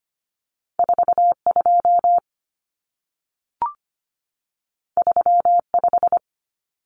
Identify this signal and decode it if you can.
Morse code, heard as a keyed tone, about 25 wpm, 700 Hz. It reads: 42 35
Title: Numbers - 2-Digit 25wpm